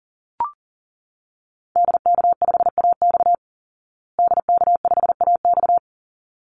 Morse code, heard as a keyed tone, about 40 wpm, 700 Hz. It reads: DK5AX DK5AX
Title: Call Signs Repeated 3x - 40wpm